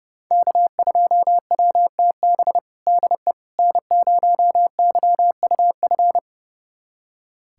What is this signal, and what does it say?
Morse code, heard as a keyed tone, about 30 wpm, 700 Hz. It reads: K2WTB DE N0YUF
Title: USA Callsign DE Callsign - 5 Characters - 30wpm